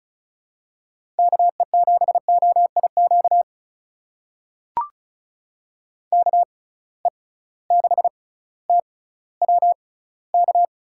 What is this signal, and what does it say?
Morse code, heard as a keyed tone, about 36 wpm, spaced as for 18 wpm, 700 Hz. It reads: KE7OIQ KE6TWK
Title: USA Call Signs - 6 Characters - 6x Character Spacing - 35wpm